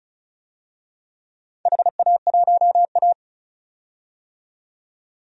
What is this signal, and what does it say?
Morse code, heard as a keyed tone, about 36 wpm, 690 Hz. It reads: HA1A
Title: DXCC-100 Call Signs - 4 Characters - 35wpm